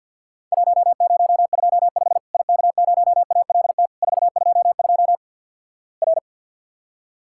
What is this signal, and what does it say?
Morse code, heard as a keyed tone, about 50 wpm, 700 Hz. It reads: 1025 IK0ALT 422 R
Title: Worked All Europe DX Contest - QTC Exchange - 50wpm